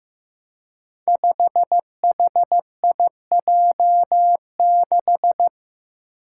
Morse code, heard as a keyed tone, about 15 wpm, 700 Hz. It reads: 5HIJ6
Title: Five Challenging Letter-Number Combinations 15wpm